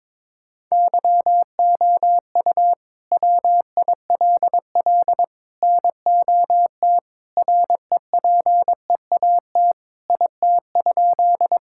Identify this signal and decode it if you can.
Morse code, heard as a keyed tone, about 22 wpm, 700 Hz. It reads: YOU WILL NOT REPEAT IT?
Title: Sentences from Top 1600 Words 22wpm